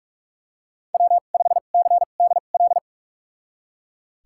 Morse code, heard as a keyed tone, about 41 wpm, spaced as for 31 wpm, 700 Hz. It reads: W5CDL
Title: USA Call Signs - 5 Characters - 2x Character Spacing - 45wpm